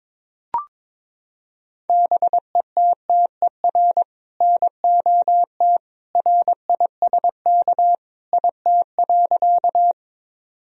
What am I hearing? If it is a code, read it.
Morse code, heard as a keyed tone, about 21 wpm, 700 Hz. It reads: BETTER NOT RISK IT.
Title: Sentences from Top 1000 Words - Review Part 4 - 22wpm